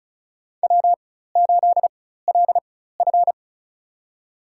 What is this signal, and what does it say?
Morse code, heard as a keyed tone, about 34 wpm, spaced as for 16 wpm, 700 Hz. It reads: W8LF
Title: USA Call Signs - 4 Characters - 4x Character Spacing - 35wpm